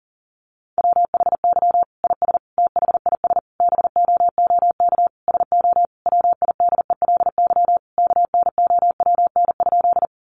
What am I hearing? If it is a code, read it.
Morse code, heard as a keyed tone, about 40 wpm, 700 Hz. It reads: WHY IS THIS BOOK SO WIDELY KNOWN?
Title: Sentences from Top 1200 Words 40wpm